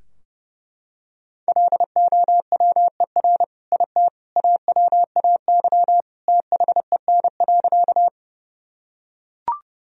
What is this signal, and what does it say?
Morse code, heard as a keyed tone, about 30 wpm, 700 Hz. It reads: LOWER IT AWAY THEN.
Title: Sentences from Top 1000 Words - Review Part 4 - 30wpm